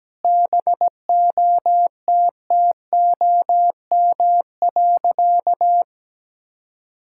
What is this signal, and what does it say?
Morse code, heard as a keyed tone, about 17 wpm, 700 Hz. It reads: BOTTOM.
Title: Sentences from Top 1000 Words - Review Part 6 - 17wpm